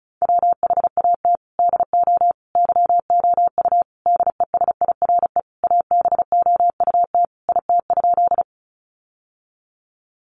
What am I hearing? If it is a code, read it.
Morse code, heard as a keyed tone, about 34 wpm, 700 Hz. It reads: WHAT DO YOU DESIRE ABOUT IT?
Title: Sentences from Top 1100 Words 35wpm